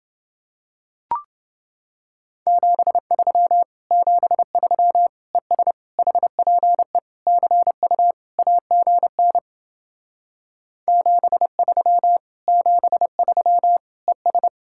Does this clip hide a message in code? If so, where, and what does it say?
Yes: Morse code from 2.5 to 14.5 s, heard as a keyed tone, about 28 wpm, 700 Hz. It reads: 73 73 ES HPE CU AGN 73 73 ES